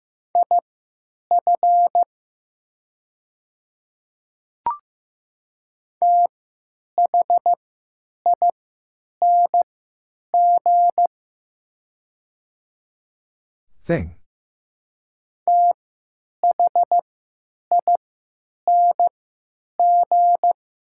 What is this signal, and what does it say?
Morse code, heard as a keyed tone, about 15 wpm, spaced as for 9 wpm, 700 Hz. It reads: IF THING THING
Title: 3x Character Spacing - Top 200 Words - 15wpm